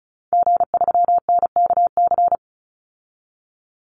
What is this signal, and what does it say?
Morse code, heard as a keyed tone, about 34 wpm, 700 Hz. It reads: G3NKC
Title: Call Signs Repeated 3x - 35wpm